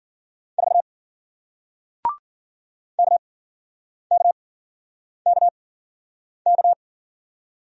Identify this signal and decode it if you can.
Morse code, heard as a keyed tone, about 51 wpm, spaced as for 9 wpm, 700 Hz. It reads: 4 KKKK